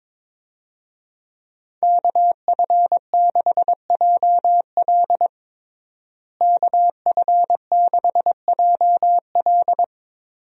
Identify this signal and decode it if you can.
Morse code, heard as a keyed tone, about 22 wpm, 700 Hz. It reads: KF6JL KF6JL
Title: USA Call Signs - 5 Characters - Repeated 3x - 22wpm